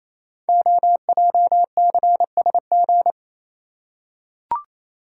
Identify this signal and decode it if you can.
Morse code, heard as a keyed tone, about 28 wpm, 700 Hz. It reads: OJCSG
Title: Code Groups - 5 Letters - 28wpm